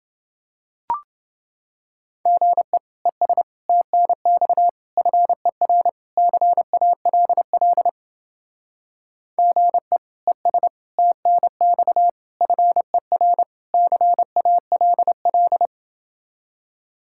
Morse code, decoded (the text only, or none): GE ES TNX FER CALL GE ES TNX FER CALL